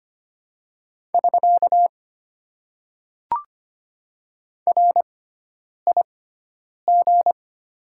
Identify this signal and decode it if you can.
Morse code, heard as a keyed tone, about 25 wpm, spaced as for 9 wpm, 700 Hz. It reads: <SK> RIG
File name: Top 90 QSO Elements - 6x Character Spacing - 25wpm